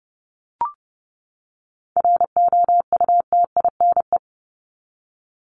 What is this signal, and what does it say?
Morse code, heard as a keyed tone, about 30 wpm, 700 Hz. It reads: ROUTINE
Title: Top 1501-1750 Words 30wpm